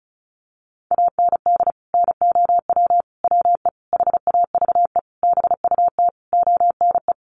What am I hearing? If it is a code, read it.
Morse code, heard as a keyed tone, about 34 wpm, 700 Hz. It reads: AND NOW WE HAVE BUT ONE
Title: Sentences from Top 1500 Words 35wpm